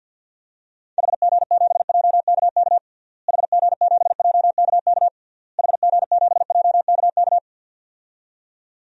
Morse code, heard as a keyed tone, about 53 wpm, 700 Hz. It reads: HG7JKK HG7JKK HG7JKK
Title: DXCC-100 Call Signs - 6 Characters - Repeated 3x - 50wpm